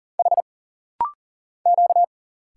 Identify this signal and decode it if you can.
Morse code, heard as a keyed tone, about 40 wpm, 700 Hz. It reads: H Q
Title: Single Letters - Rapid-Fire 40wpm